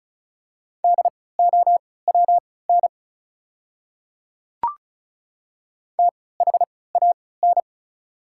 Morse code, heard as a keyed tone, about 36 wpm, spaced as for 20 wpm, 690 Hz. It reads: DOWN THAN
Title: 3x Character Spacing - Top 100 Words - 35wpm